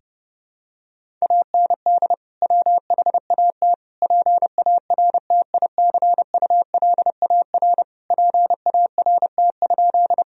Morse code, heard as a keyed tone, about 30 wpm, 700 Hz. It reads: AND WHAT PARTICULAR PART?
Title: Sentences from Top 1000 Words - Review Part 4 - 30wpm